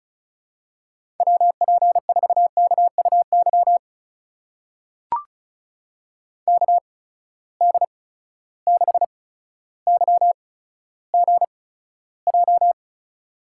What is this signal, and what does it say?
Morse code, heard as a keyed tone, about 36 wpm, spaced as for 15 wpm, 700 Hz. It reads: WP4KUY KD6YGJ
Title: USA Call Signs - 8x Character Spacing - 35wpm